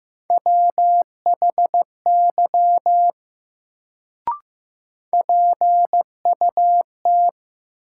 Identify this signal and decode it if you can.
Morse code, heard as a keyed tone, about 15 wpm, 700 Hz. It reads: WHY PUT
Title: Sentences from Top 200 Words - New Word Drill - 15wpm